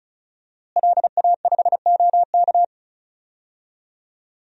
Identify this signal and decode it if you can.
Morse code, heard as a keyed tone, about 34 wpm, 700 Hz. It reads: LA5OK
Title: DXCC-100 Call Signs - 5 Characters - Repeated 3x - 35wpm